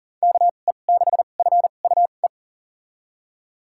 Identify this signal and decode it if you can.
Morse code, heard as a keyed tone, about 40 wpm, spaced as for 29 wpm, 700 Hz. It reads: KE6FUE
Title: USA Call Signs - 6 Characters - 2x Character Spacing - 40wpm